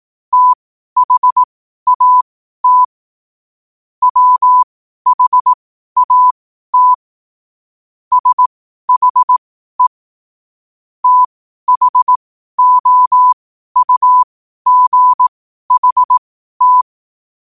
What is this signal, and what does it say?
Morse code, heard as a keyed tone, about 18 wpm, spaced as for 12 wpm, 1000 Hz. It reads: THAT WHAT SHE THOUGHT
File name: qotdfiles / 12wpm